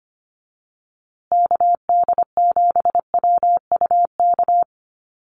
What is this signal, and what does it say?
Morse code, heard as a keyed tone, about 25 wpm, 700 Hz. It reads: KD7WUK
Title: USA Call Signs Repeated 3x 25wpm